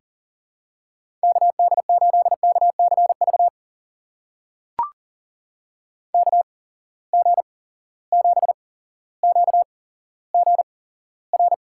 Morse code, heard as a keyed tone, about 39 wpm, spaced as for 17 wpm, 700 Hz. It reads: KD8KCV KG7QGR
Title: USA Call Signs - 6 Characters - 8x Character Spacing - 40wpm